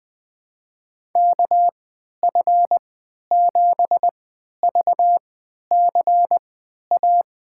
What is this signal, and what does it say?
Morse code, heard as a keyed tone, about 20 wpm, spaced as for 11 wpm, 700 Hz. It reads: KF7VCA
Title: USA Call Signs - 6 Characters - 3x Character Spacing - 20wpm